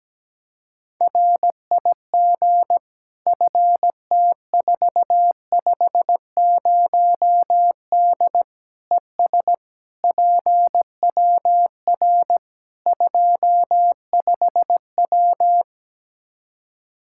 Morse code, heard as a keyed tone, about 17 wpm, 700 Hz. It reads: RIG FT450D ES PWR 25W
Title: Practice QSO - Exchange 2 - Info - 17wpm